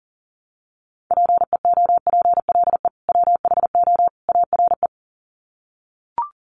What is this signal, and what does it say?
Morse code, heard as a keyed tone, about 40 wpm, 700 Hz. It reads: PEOPLE WHO ARE
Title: Sets of 3 Words 40wpm